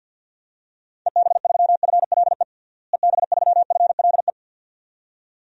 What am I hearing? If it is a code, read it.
Morse code, heard as a keyed tone, about 51 wpm, 700 Hz. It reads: EB3FLE EB3FLE
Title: DXCC-100 Call Signs - 6 Characters - Repeated 2x - 50wpm